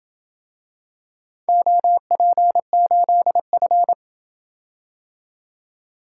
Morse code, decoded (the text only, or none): OP8F